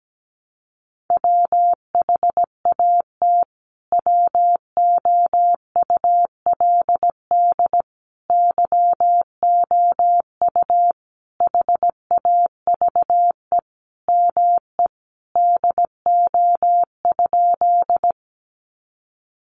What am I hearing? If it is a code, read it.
Morse code, heard as a keyed tone, about 17 wpm, 700 Hz. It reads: WHAT WOULD YOU HAVE ME DO?
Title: Sentences from Top 200 Words 17wpm